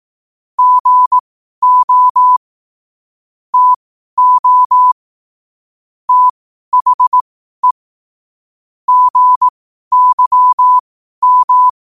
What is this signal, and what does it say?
Morse code, heard as a keyed tone, about 18 wpm, spaced as for 12 wpm, 1000 Hz. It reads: GO TO THE GYM
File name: qotdfiles / 12wpm